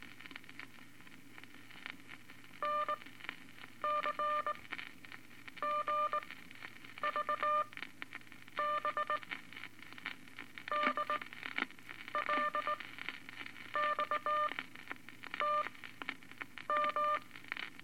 Morse code, heard as a keyed tone, about 19 wpm, spaced as for 7 wpm, 1200 Hz, with noise. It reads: NCGVB DLXTM
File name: Eight Words Per Minute